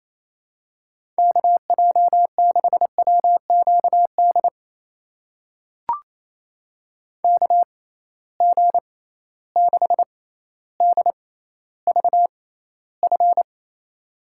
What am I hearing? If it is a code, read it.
Morse code, heard as a keyed tone, about 28 wpm, spaced as for 14 wpm, 700 Hz. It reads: KJ6WQD KG6DVF
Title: USA Call Signs - 6 Characters - 6x Character Spacing - 28wpm